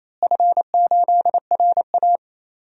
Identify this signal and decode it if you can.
Morse code, heard as a keyed tone, about 28 wpm, 700 Hz. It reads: F8RA
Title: DXCC-100 Call Signs - 4 Characters - Repeated 2x - 28wpm